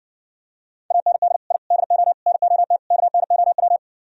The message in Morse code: AND I DO NOT KNOW

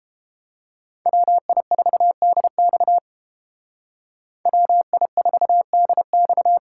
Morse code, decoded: WI4DX WI4DX

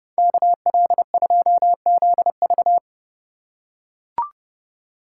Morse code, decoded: KL2ZV

